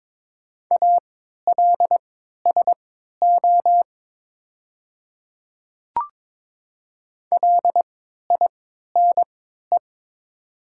Morse code, heard as a keyed tone, about 21 wpm, spaced as for 12 wpm, 700 Hz. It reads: ALSO LINE
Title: 3x Character Spacing - Top 200 Words - 22wpm